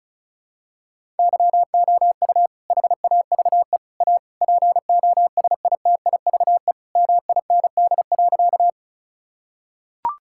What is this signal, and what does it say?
Morse code, heard as a keyed tone, about 36 wpm, 700 Hz. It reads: YOU HAVE A POSITIVE MIND.
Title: Sentences from Top 1000 Words - Review Part 5 - 35wpm